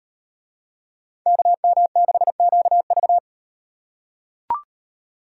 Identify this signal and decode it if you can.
Morse code, heard as a keyed tone, about 38 wpm, 700 Hz. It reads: KM6QV